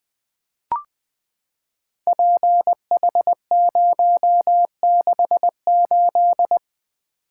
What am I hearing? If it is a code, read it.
Morse code, heard as a keyed tone, about 20 wpm, 700 Hz. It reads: PH068
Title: Code Groups - 5 Letter:Numbers - 20wpm